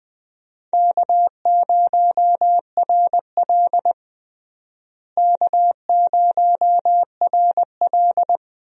Morse code, heard as a keyed tone, about 20 wpm, 700 Hz. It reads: K0RL K0RL
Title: USA Call Signs - 4 Characters - Repeated 2x - 20wpm